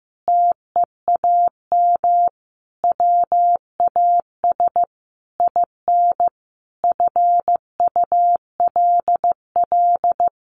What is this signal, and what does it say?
Morse code, heard as a keyed tone, about 14 wpm, 700 Hz. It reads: TEAM WAS IN FULL
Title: Sentences from Top 1500 Words 15wpm